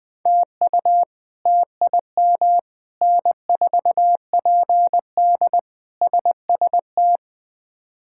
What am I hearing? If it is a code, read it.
Morse code, heard as a keyed tone, about 20 wpm, 700 Hz. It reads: TU TIM N4PD SST
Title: K1USN SST QSO Practice - 20wpm